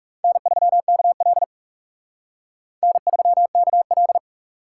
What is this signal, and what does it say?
Morse code, heard as a keyed tone, about 35 wpm, 700 Hz. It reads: N3KL N3KL
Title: Speed Racing - USA 1x2 Call Signs 30wpm